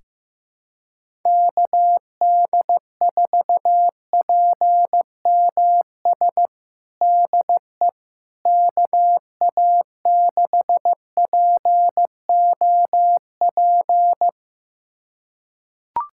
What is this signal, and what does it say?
Morse code, heard as a keyed tone, about 16 wpm, 700 Hz. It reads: KD4PMS DE KA6POP
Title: USA Callsign DE Callsign 15wpm